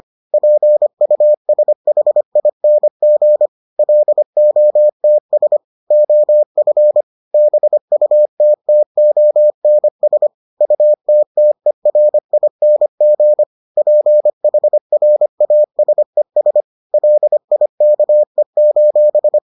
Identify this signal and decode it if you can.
Morse code, heard as a keyed tone, about 25 wpm, 600 Hz. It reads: PUSHING LOTS OF BUTTONS UTTERING PHRASES LIKE: